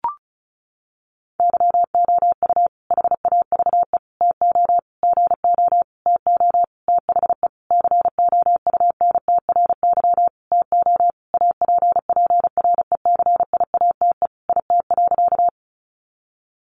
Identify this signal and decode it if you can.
Morse code, heard as a keyed tone, about 34 wpm, 700 Hz. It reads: YOU HAVE TO GO TO THE COUNTRY TO APPRECIATE IT.
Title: Sentences from Top 1300 Words 35wpm